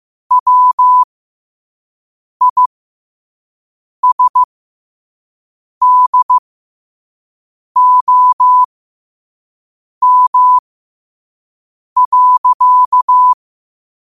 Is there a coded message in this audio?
Yes: Morse code from 0.3 to 13.3 s, heard as a keyed tone, about 15 wpm, spaced as for 5 wpm, 1000 Hz. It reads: WISDOM.